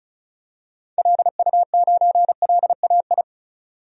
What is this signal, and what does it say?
Morse code, heard as a keyed tone, about 34 wpm, 700 Hz. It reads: LU9LAI
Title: DXCC-100 Call Signs - 6 Characters - 35wpm